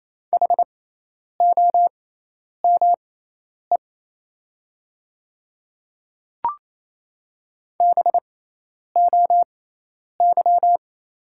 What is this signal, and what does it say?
Morse code, heard as a keyed tone, about 28 wpm, spaced as for 10 wpm, 700 Hz. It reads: HOME BOY
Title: 6x Character Spacing - Top 200 Words - 28wpm